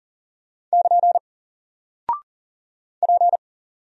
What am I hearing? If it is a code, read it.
Morse code, heard as a keyed tone, about 40 wpm, 700 Hz. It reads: <KN> P
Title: Single Character (Letter:Number:Punctuation:Prosign) 40wpm